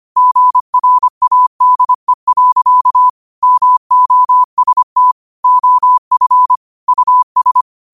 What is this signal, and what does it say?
Morse code, heard as a keyed tone, about 25 wpm, 1000 Hz. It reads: GRADE. MOST OF US
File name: qotdfiles / 25wpm